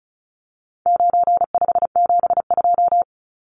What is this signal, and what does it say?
Morse code, heard as a keyed tone, about 34 wpm, 700 Hz. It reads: 9572
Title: Numbers - 4-Digit 35wpm